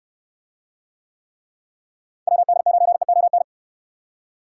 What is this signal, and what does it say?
Morse code, heard as a keyed tone, about 69 wpm, 700 Hz. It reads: WD9EON